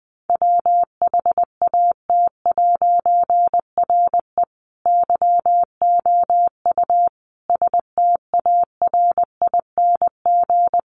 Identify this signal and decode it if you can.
Morse code, heard as a keyed tone, about 20 wpm, 700 Hz. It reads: WHAT'RE YOU STARING
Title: A Sound Of Thunder 20wpm